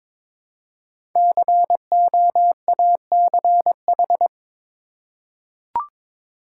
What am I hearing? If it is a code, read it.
Morse code, heard as a keyed tone, about 23 wpm, 700 Hz. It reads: COACH
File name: 5 Letter Words - First 3 Letters Given - 22wpm